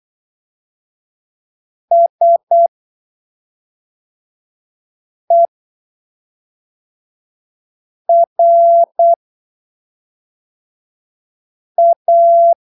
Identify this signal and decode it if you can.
Morse code, heard as a keyed tone, about 8 wpm, spaced as for 3 wpm, 680 Hz. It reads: SERA
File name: arquivos / texto / texto 28 08ppm